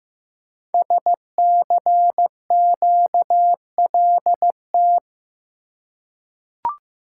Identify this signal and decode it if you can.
Morse code, heard as a keyed tone, about 15 wpm, 700 Hz. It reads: SCQLT
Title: Code Groups - 5 Letters - 15wpm